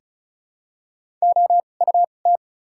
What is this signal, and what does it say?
Morse code, heard as a keyed tone, about 36 wpm, spaced as for 25 wpm, 700 Hz. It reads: OUT
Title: 2x Character Spacing - Top 100 Words - 35wpm